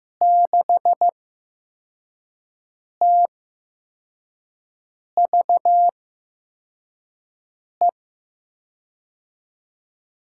Morse code, heard as a keyed tone, about 15 wpm, spaced as for 4 wpm, 700 Hz. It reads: 6TVE
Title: USA Call Signs - 6 Characters - 8x Character Spacing - 15wpm